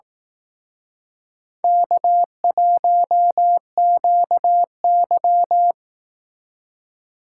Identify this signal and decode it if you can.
Morse code, heard as a keyed tone, about 18 wpm, 700 Hz. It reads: K1QY
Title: Speed Racing - USA 1x2 Call Signs 15wpm